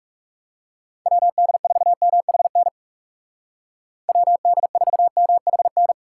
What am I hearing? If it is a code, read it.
Morse code, heard as a keyed tone, about 37 wpm, 700 Hz. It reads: WD4MHN WD4MHN
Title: Speed Racing - USA 2x3 Call Signs 30wpm